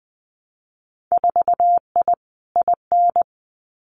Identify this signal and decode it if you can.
Morse code, heard as a keyed tone, about 20 wpm, 700 Hz. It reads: 4I IN
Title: Winter Field Day Exchanges Encore 20wpm